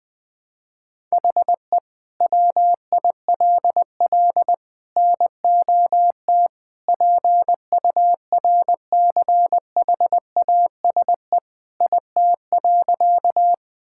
Morse code, heard as a keyed tone, about 20 wpm, 700 Hz. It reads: HE WILL NOT PURCHASE IT.
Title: Sentences from Top 1000 Words - Review Part 5 - 20wpm